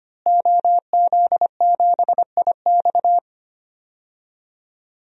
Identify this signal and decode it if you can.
Morse code, heard as a keyed tone, about 25 wpm, 700 Hz. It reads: OZ7IX
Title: DXCC-100 Call Signs - 5 Characters - Repeated 3x - 25wpm